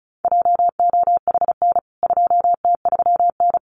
Morse code, heard as a keyed tone, about 34 wpm, 700 Hz. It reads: JOHN 2T3N
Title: CWT Exchange 35wpm